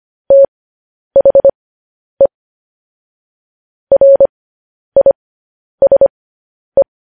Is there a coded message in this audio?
Yes: Morse code from 0.3 to 6.8 s, heard as a keyed tone, about 25 wpm, spaced as for 10 wpm, 550 Hz. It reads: THE RISE